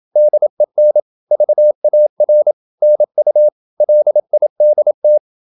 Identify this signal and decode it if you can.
Morse code, heard as a keyed tone, about 27 wpm, 600 Hz. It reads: DEN VAR NU LIDT
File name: Ferie QRQ 27wpm